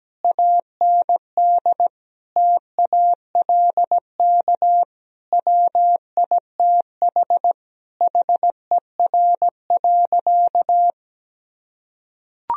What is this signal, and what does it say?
Morse code, heard as a keyed tone, about 17 wpm, 700 Hz. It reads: AND TALK WITH HER.
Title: Sentences from Top 1000 Words - Review Part 4 - 17wpm